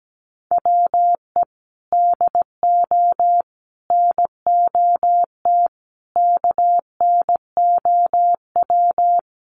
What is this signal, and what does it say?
Morse code, heard as a keyed tone, about 17 wpm, 700 Hz. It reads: WE DO NOT KNOW
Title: Sets of 5 Words Encore 17wpm